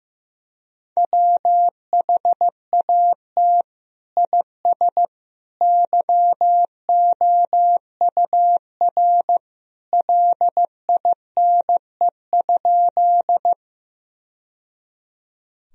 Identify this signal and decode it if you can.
Morse code, heard as a keyed tone, about 15 wpm, 700 Hz. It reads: WHAT IS YOUR LINE?
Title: Sentences from Top 1000 Words - Review Part 1 - 15wpm